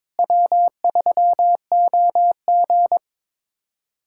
Morse code, heard as a keyed tone, about 23 wpm, 700 Hz. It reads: W3OG
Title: USA Callsign DE Callsign - 4 Characters - 22wpm